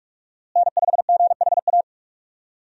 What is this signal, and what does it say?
Morse code, heard as a keyed tone, about 41 wpm, 700 Hz. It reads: N5GHA
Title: USA Call Signs - 5 Characters - Repeated 2x - 45wpm